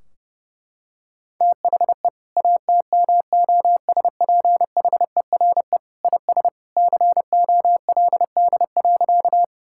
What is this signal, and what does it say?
Morse code, heard as a keyed tone, about 30 wpm, 700 Hz. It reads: THE ATMOSPHERE IS COLD.
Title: Sentences from Top 1600 Words 30wpm